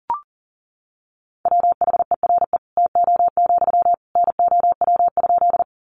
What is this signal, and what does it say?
Morse code, heard as a keyed tone, about 40 wpm, 700 Hz. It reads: WHERE TO, NOW?
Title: Sentences from Top 200 Words 40wpm